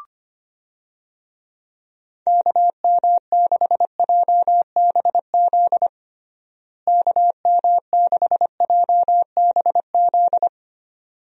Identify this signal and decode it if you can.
Morse code, heard as a keyed tone, about 25 wpm, 700 Hz. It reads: KM6JBZ KM6JBZ
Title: USA Call Signs - 6 Characters - Repeated 2x - 25wpm